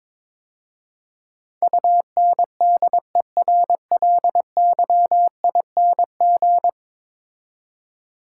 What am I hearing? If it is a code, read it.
Morse code, heard as a keyed tone, about 21 wpm, 700 Hz. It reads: UNDERLYING
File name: practice / 22wpm / 10 Letter Words - First 3 Letters Given - 22wpm